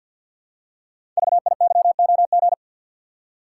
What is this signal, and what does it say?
Morse code, heard as a keyed tone, about 53 wpm, 700 Hz. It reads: VIYOG
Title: Code Groups - 5 Letters - 50wpm